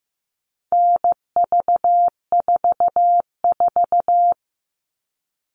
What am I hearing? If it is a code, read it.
Morse code, heard as a keyed tone, about 15 wpm, 700 Hz. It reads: NV44
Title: All Cut Numbers - 4-Digit 15wpm